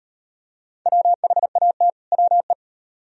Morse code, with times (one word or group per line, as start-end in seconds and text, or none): 0.9-1.9 WHAT
2.1-2.5 WE